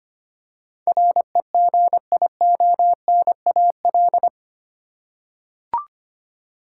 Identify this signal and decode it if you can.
Morse code, heard as a keyed tone, about 25 wpm, 700 Hz. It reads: REGIONAL
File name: 8 Letter Words - First 3 Letters Given - 25wpm